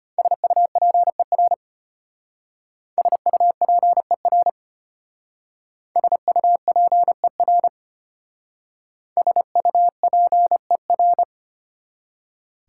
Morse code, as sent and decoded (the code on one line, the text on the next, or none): ... ..- .--. .-. / ... ..- .--. .-. / ... ..- .--. .-. / ... ..- .--. .-.
SUPER SUPER SUPER SUPER